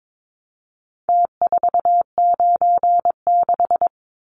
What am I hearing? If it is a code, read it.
Morse code, heard as a keyed tone, about 22 wpm, 700 Hz. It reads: T496
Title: Numbers - Contest Style - 4-Digit 22wpm